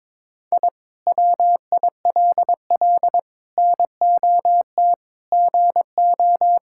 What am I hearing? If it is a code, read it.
Morse code, heard as a keyed tone, about 22 wpm, 700 Hz. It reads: I WILL NOT GO
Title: Sentences from Top 1000 Words - Review Part 6 - 22wpm